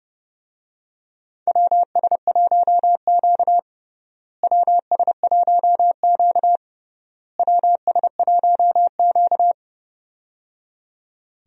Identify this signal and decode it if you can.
Morse code, heard as a keyed tone, about 30 wpm, 700 Hz. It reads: WS1Q WS1Q WS1Q
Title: USA Call Signs - 4 Characters - Repeated 3x - 30wpm